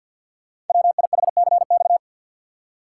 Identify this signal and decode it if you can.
Morse code, heard as a keyed tone, about 48 wpm, 690 Hz. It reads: WIHCX
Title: Code Groups - 5 Letters - 50wpm